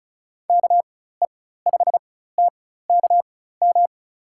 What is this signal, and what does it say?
Morse code, heard as a keyed tone, about 36 wpm, spaced as for 16 wpm, 700 Hz. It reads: KE5TKM
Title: USA Call Signs - 6 Characters - 4x Character Spacing - 35wpm